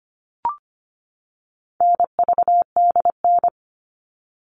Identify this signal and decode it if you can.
Morse code, heard as a keyed tone, about 25 wpm, 700 Hz. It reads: NVDN
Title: All Cut Numbers - 4-Digit 25wpm